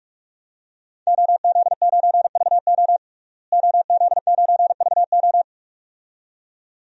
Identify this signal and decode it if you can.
Morse code, heard as a keyed tone, about 46 wpm, 700 Hz. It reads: OZ9VO OZ9VO
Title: DXCC-100 Call Signs - 5 Characters - Repeated 2x - 45wpm